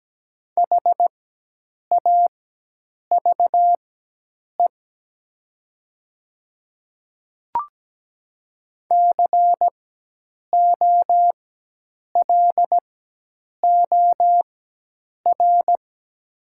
Morse code, heard as a keyed tone, about 17 wpm, spaced as for 8 wpm, 700 Hz. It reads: HAVE COLOR
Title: 4x Character Spacing - Top 300 Words - 17wpm